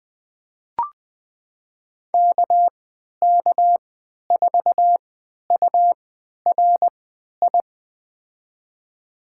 Morse code, heard as a keyed tone, about 20 wpm, spaced as for 11 wpm, 700 Hz. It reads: KK4URI
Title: USA Call Signs - 6 Characters - 3x Character Spacing - 20wpm